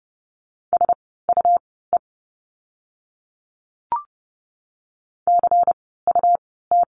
Morse code, heard as a keyed tone, about 30 wpm, spaced as for 17 wpm, 700 Hz. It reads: SUE CUT
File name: Lesson 17f - Letter u - Words 30wpm